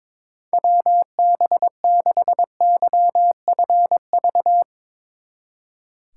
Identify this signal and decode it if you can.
Morse code, heard as a keyed tone, about 22 wpm, 700 Hz. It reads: WB6YFV
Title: Speed Racing - USA 2x3 Call Signs 22wpm